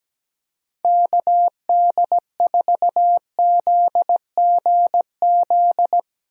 Morse code, heard as a keyed tone, about 17 wpm, 700 Hz. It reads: KD4ZGZ
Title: USA Call Signs - 6 Characters - Repeated 3x - 17wpm